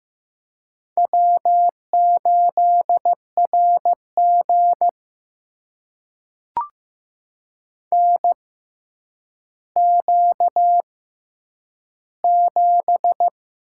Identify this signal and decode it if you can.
Morse code, heard as a keyed tone, about 15 wpm, spaced as for 8 wpm, 700 Hz. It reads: W8RG NQ7
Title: USA Call Signs - 6x Character Spacing - 15wpm